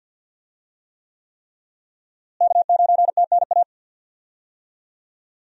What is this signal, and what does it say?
Morse code, heard as a keyed tone, about 52 wpm, 700 Hz. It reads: K9TNA
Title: USA Call Signs - 5 Characters - Repeated 3x - 50wpm